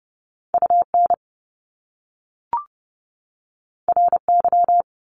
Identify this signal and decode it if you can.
Morse code, heard as a keyed tone, about 30 wpm, 700 Hz. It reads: UN RY
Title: Top Two Letter Combinations in Words 30wpm